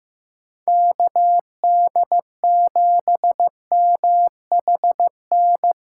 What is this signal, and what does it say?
Morse code, heard as a keyed tone, about 15 wpm, 700 Hz. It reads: KD7MHN
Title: USA Call Signs - 2x Character Spacing - 15wpm